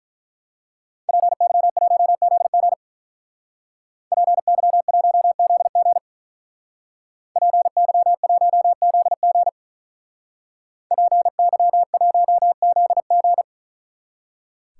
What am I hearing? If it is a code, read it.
Morse code, heard as a keyed tone, about 43 wpm, 700 Hz. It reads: PY1ZG PY1ZG PY1ZG PY1ZG
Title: Speed Racing - USA 2x2 Call Signs 35wpm